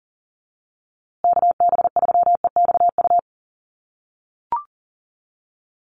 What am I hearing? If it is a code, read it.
Morse code, heard as a keyed tone, about 40 wpm, 700 Hz. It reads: KB3EXU
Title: USA Call Signs Repeated 2x 40wpm